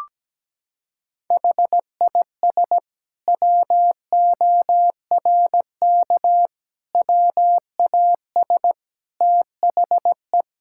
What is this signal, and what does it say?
Morse code, heard as a keyed tone, about 17 wpm, 700 Hz. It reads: HIS WORK WAS THE
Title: Sentences from Top 1000 Words - Review Part 5 - 17wpm